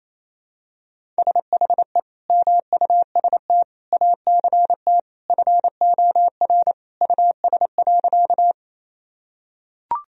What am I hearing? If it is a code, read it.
Morse code, heard as a keyed tone, about 28 wpm, 700 Hz. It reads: SHE MUST ACT FOR US.
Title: Sentences from Top 1000 Words - Review Part 1 - 28wpm